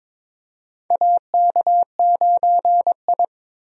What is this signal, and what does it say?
Morse code, heard as a keyed tone, about 22 wpm, 700 Hz. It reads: AK9I
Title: USA Call Signs - 4 Characters - Repeated 3x - 22wpm